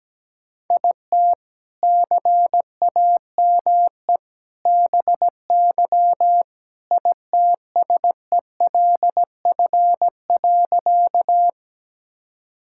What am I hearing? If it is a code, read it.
Morse code, heard as a keyed tone, about 17 wpm, 700 Hz. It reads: IT CAME BY ITSELF.
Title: Sentences from Top 1000 Words - Review Part 5 - 17wpm